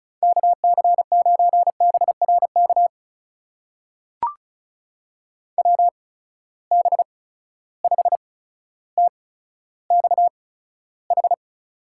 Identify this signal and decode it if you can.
Morse code, heard as a keyed tone, about 36 wpm, spaced as for 15 wpm, 700 Hz. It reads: KC9BRK WB5TXH
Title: USA Call Signs - 6 Characters - 8x Character Spacing - 35wpm